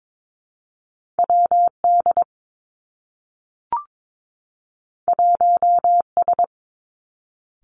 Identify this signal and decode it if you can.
Morse code, heard as a keyed tone, about 22 wpm, 700 Hz. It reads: W D 1 S
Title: Two Challenging Letter-Number Combinations 22wpm